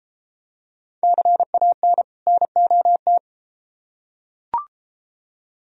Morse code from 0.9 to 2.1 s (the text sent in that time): CAN